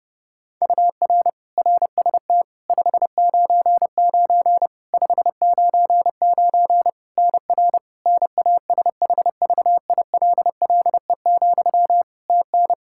Morse code, heard as a keyed tone, about 30 wpm, 700 Hz. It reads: UR RST 599 599 NR NASHVILLE, TN
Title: Practice QSO - Exchange 1 - Beginning - 30wpm